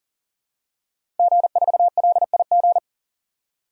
Morse code, heard as a keyed tone, about 40 wpm, 700 Hz. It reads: G4LIG
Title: DXCC-100 Call Signs - 5 Characters - 40wpm